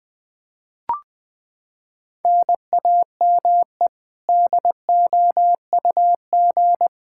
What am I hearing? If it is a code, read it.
Morse code, heard as a keyed tone, about 20 wpm, 700 Hz. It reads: NAME DOUG
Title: Practice QSO - Exchange 1 - Beginning - 20wpm